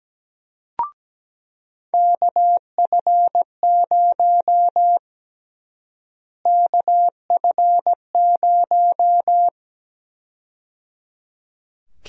Morse code, heard as a keyed tone, about 17 wpm, 700 Hz. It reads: KF0 KF0
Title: USA Call Sign Prefixes - Repeated 2x - 17wpm